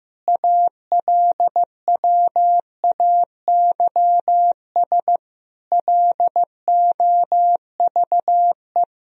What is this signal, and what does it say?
Morse code, heard as a keyed tone, about 15 wpm, 700 Hz. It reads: ALWAYS LOVE
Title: Sentences from Top 400 Words and Abbreviations 15wpm